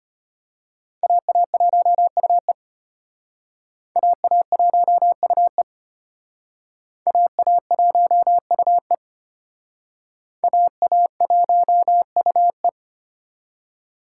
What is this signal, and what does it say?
Morse code, heard as a keyed tone, about 31 wpm, 700 Hz. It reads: AA1UE AA1UE AA1UE AA1UE